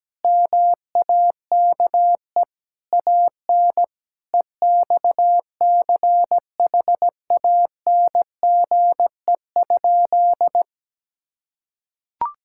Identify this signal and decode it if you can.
Morse code, heard as a keyed tone, about 17 wpm, 700 Hz. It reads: MAKE AN EXCHANGE?
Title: Sentences from Top 1000 Words - Review Part 6 - 17wpm